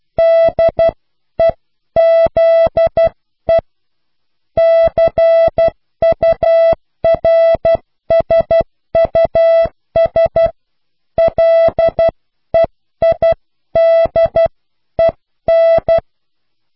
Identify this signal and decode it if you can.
Morse code, heard as a keyed tone, about 12 wpm, 660 Hz. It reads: DEZE CURSUS LEIDEN